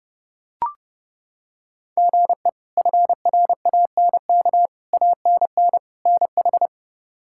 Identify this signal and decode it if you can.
Morse code, heard as a keyed tone, about 30 wpm, 700 Hz. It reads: GE FRANK ANN NH
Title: K1USN SST QSO Practice - 30wpm